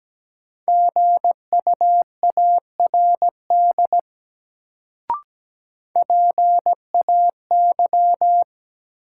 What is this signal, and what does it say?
Morse code, heard as a keyed tone, about 17 wpm, 700 Hz. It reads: GUARD PAY
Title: Sentences from Top 700 Words - New Word Drill - 17wpm